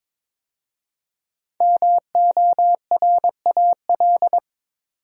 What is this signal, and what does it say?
Morse code, heard as a keyed tone, about 22 wpm, 700 Hz. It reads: MORAL
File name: 5 Letter Words - First 3 Letters Given - 22wpm